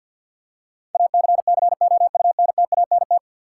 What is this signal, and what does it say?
Morse code, heard as a keyed tone, about 47 wpm, 700 Hz. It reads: ACCOUNTANT
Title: Practice QSO - Exchange 3 - More Info - 50wpm